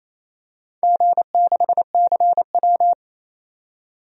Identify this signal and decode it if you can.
Morse code, heard as a keyed tone, about 29 wpm, 700 Hz. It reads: G6CW
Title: DXCC-100 Call Signs - 4 Characters - 28wpm